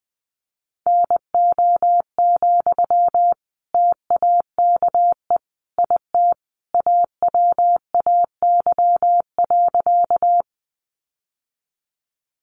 Morse code, heard as a keyed tone, about 20 wpm, 700 Hz. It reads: NO, TAKE IT AWAY.
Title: Sentences from Top 600 Words 20wpm